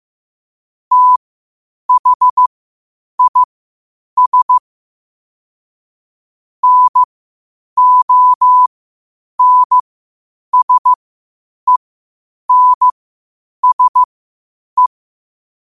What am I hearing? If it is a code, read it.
Morse code, heard as a keyed tone, about 15 wpm, spaced as for 8 wpm, 1000 Hz. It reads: THIS NONSENSE
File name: qotdfiles / 8wpm